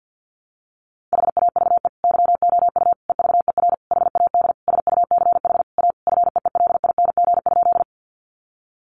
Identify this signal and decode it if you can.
Morse code, heard as a keyed tone, about 50 wpm, 700 Hz. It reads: HAVE YOU EVER HAD SUCH A FEELING?
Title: Sentences from Top 1200 Words 50wpm